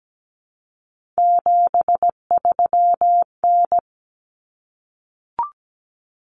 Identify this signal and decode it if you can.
Morse code, heard as a keyed tone, about 17 wpm, 700 Hz. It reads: 73N
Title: Common Cut Numbers - 3-Digit 17wpm